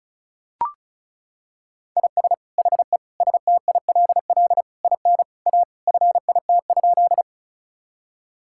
Morse code, heard as a keyed tone, about 36 wpm, 700 Hz. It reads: IS HE STILL IN A FIT?
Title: Sentences from Top 1000 Words - Review Part 5 - 35wpm